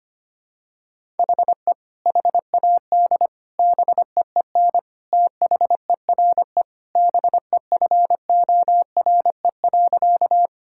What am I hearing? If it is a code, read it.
Morse code, heard as a keyed tone, about 25 wpm, 700 Hz. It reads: HE HAD BEEN THERE BEFORE.
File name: practice / 25wpm / Sentences with 3-Word Phrases - 25wpm